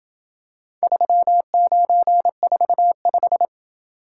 Morse code, heard as a keyed tone, about 29 wpm, 700 Hz. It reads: 3945